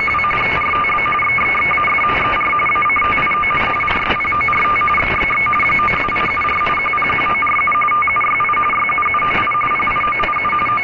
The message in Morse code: none